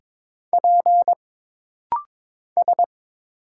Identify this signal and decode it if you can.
Morse code, heard as a keyed tone, about 22 wpm, 700 Hz. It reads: P S
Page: This rapid-fire timing is geared to push your ICR (Instant Character Recognition) skills.